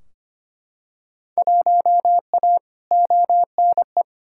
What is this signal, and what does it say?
Morse code, heard as a keyed tone, about 25 wpm, 700 Hz. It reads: 1A ONE
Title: Field Day Exchanges Encore - 25wpm